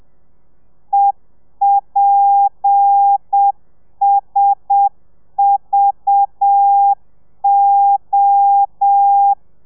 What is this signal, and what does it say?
Morse code, heard as a keyed tone, about 7 wpm, 800 Hz. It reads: EPSVO
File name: telegrafia / i2tfj / Lez7 30